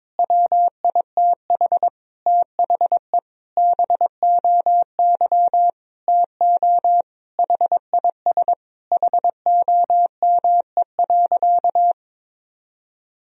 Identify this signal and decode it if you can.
Morse code, heard as a keyed tone, about 22 wpm, 700 Hz. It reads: WITH THE BOY TO HIS HOME.
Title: Sentences from Top 1000 Words - Review Part 1 - 22wpm